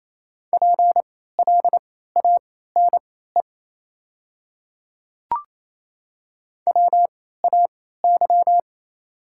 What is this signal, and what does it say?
Morse code, heard as a keyed tone, about 28 wpm, spaced as for 16 wpm, 690 Hz. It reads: PLANE WAY
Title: 3x Character Spacing - Top 500 Words - 28wpm